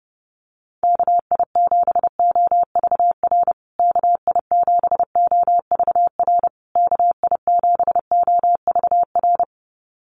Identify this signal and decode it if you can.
Morse code, heard as a keyed tone, about 30 wpm, 700 Hz. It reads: KI7OVR KI7OVR KI7OVR
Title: USA Call Signs Repeated 3x 30wpm